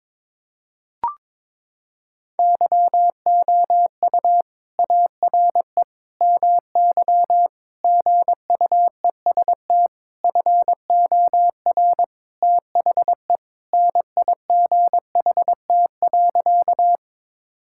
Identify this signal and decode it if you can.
Morse code, heard as a keyed tone, about 22 wpm, 700 Hz. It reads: YOU ARE MY GUEST FOR THE NIGHT.
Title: Sentences from Top 1600 Words 22wpm